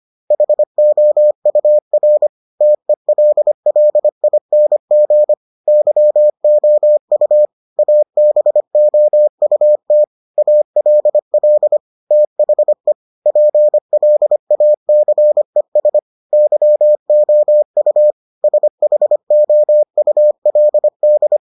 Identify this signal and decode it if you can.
Morse code, heard as a keyed tone, about 25 wpm, 600 Hz. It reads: HOUR TELLING YOU ABOUT ALL THE PLACES YOU SHOULD